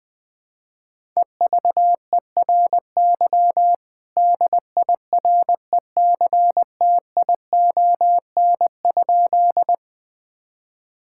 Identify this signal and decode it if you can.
Morse code, heard as a keyed tone, about 21 wpm, 700 Hz. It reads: EVERY DIRECTION?
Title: Sentences from Top 1000 Words - Review Part 5 - 20wpm